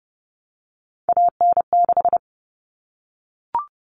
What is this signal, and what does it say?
Morse code, heard as a keyed tone, about 30 wpm, 700 Hz. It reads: AN6
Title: Common Cut Numbers - 3-Digit 30wpm